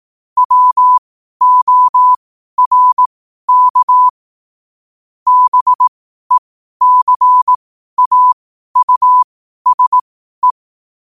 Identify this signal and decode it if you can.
Morse code, heard as a keyed tone, about 18 wpm, spaced as for 12 wpm, 1000 Hz. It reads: WORK BECAUSE
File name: qotdfiles / 12wpm